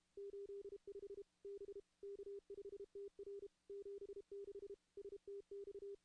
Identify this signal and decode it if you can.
Morse code, heard as a keyed tone, about 33 wpm, 400 Hz. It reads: 85 B K5TR 76 STX
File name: contests / CW-SS-k5tr- -2100